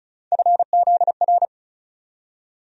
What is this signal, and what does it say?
Morse code, heard as a keyed tone, about 36 wpm, 700 Hz. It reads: FZR
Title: USA Call Sign Postfixes - 3 Letters 35wpm